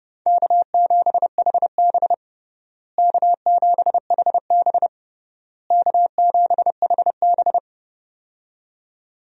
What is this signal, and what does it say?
Morse code, heard as a keyed tone, about 30 wpm, 700 Hz. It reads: K7HB K7HB K7HB
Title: USA Call Signs - 4 Characters - Repeated 3x - 30wpm